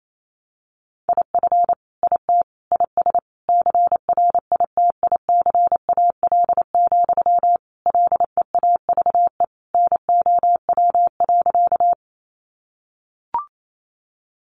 Morse code, heard as a keyed tone, about 28 wpm, 700 Hz. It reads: IF IT IS CRITICAL, LEAVE NOW.
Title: Sentences from Top 1100 Words 28wpm